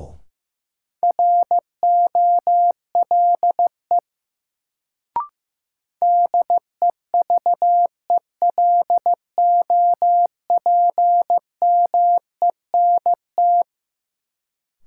Morse code, heard as a keyed tone, about 15 wpm, 700 Hz. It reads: ROLE DEVELOPMENT
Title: Sets of 3 Words - New Word Drill - 15wpm